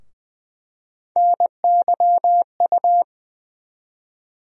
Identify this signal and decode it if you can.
Morse code, heard as a keyed tone, about 20 wpm, 700 Hz. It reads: NYU
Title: USA Call Sign Postfixes - 3 Letters 20wpm